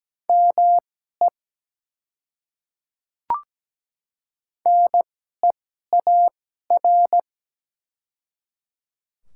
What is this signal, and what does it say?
Morse code, heard as a keyed tone, about 17 wpm, spaced as for 12 wpm, 700 Hz. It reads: ME NEAR
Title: 2x Character Spacing - Top 300 Words - 17wpm